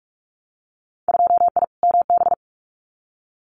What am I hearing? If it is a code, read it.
Morse code, heard as a keyed tone, about 46 wpm, 700 Hz. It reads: 2I MB
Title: Winter Field Day Exchanges Encore 45wpm